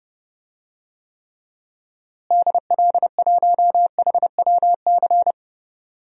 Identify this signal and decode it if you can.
Morse code, heard as a keyed tone, about 30 wpm, 700 Hz. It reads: DL1HWC
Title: DXCC-100 Call Signs - 6 Characters - Repeated 3x - 30wpm